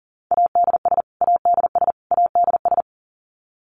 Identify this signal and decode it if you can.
Morse code, heard as a keyed tone, about 40 wpm, 700 Hz. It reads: ADS ADS ADS
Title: Abbreviations Repeated 3x - 40wpm